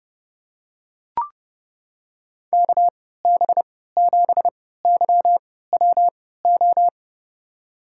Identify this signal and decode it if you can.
Morse code, heard as a keyed tone, about 30 wpm, spaced as for 17 wpm, 700 Hz. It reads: KB7YWO
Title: USA Call Signs - 6 Characters - 3x Character Spacing - 30wpm